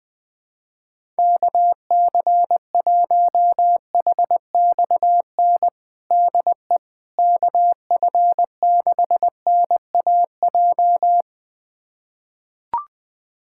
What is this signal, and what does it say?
Morse code, heard as a keyed tone, about 20 wpm, 700 Hz. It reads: KC1HXN DE KF6NAJ
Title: USA Callsign DE Callsign - 6 Characters - 20wpm